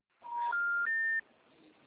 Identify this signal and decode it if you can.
Special information tone.